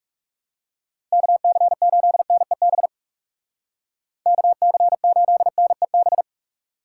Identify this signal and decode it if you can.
Morse code, heard as a keyed tone, about 36 wpm, 700 Hz. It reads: KC8NEB KC8NEB
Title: Speed Racing - USA 2x3 Call Signs 30wpm